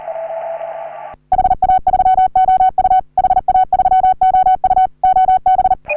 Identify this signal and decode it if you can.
Morse code, heard as a keyed tone, about 40 wpm, 750 Hz, with radio noise. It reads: HA3OU HA3OU OB
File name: CW / HA3OU RHOBCW REC